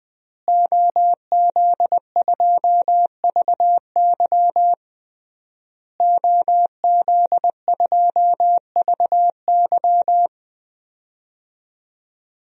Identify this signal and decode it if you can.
Morse code, heard as a keyed tone, about 20 wpm, 700 Hz. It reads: OZ2VY OZ2VY
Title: DXCC-100 Call Signs - 5 Characters - Repeated 3x - 20wpm